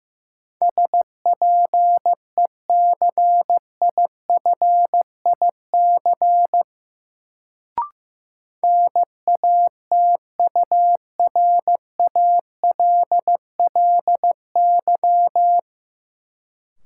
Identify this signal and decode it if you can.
Morse code, heard as a keyed tone, about 15 wpm, 700 Hz. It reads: SPECIFIC NATURALLY
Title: Sentences from Top 1300 Words - New Word Drill Alt - 15wpm